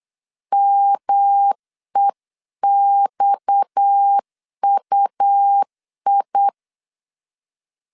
Morse code, heard as a keyed tone, about 8 wpm, 800 Hz. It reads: MEXUI
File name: LEZIONI / 6 PKX0 / 6 PKX 8wpm 3041